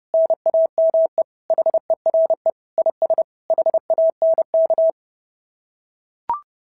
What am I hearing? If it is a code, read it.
Morse code, heard as a keyed tone, about 30 wpm, 650 Hz. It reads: NAME HERE IS HANK
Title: SOTA QSO Practice 30wpm